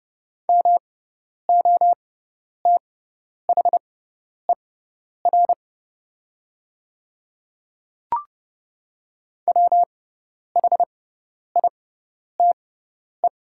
Morse code, heard as a keyed tone, about 30 wpm, spaced as for 10 wpm, 700 Hz. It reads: MOTHER WHITE
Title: 6x Character Spacing - Top 500 Words - 30wpm